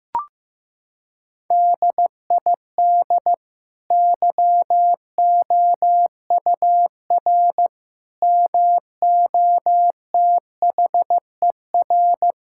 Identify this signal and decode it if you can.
Morse code, heard as a keyed tone, about 14 wpm, 700 Hz. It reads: DID YOUR MOTHER
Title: Sentences from Top 1000 Words - Review Part 6 - 15wpm